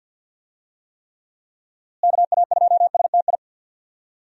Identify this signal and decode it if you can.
Morse code, heard as a keyed tone, about 46 wpm, 700 Hz. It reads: KA2STI
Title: USA Call Signs - 6 Characters - Repeated 2x - 50wpm